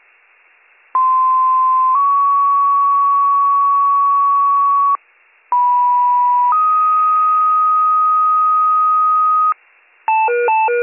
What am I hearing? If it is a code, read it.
Two-tone page, heard as a one-second tone, then a three-second tone, about 1040 then 1120 Hz, about 980 then 1250 Hz. It is heard from 0.9 to 9.5 s.